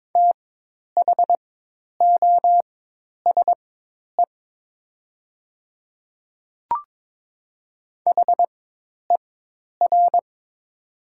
Morse code, heard as a keyed tone, about 22 wpm, spaced as for 10 wpm, 700 Hz. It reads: THOSE HER
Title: 4x Character Spacing - Top 500 Words - 22wpm